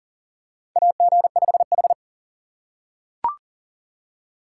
Morse code, heard as a keyed tone, about 37 wpm, 700 Hz. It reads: AG5H
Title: USA Call Signs - 4 Characters - 6x Character Spacing - 40wpm